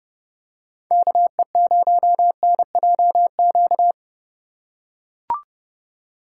Morse code, heard as a keyed tone, about 29 wpm, 700 Hz. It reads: KE0NJQ